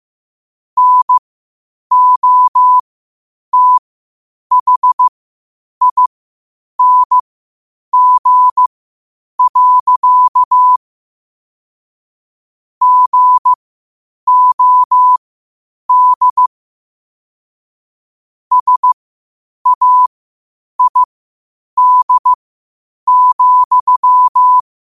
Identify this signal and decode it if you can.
Morse code, heard as a keyed tone, about 15 wpm, spaced as for 8 wpm, 1000 Hz. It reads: NOTHING. GOD SAID,
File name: 8wpm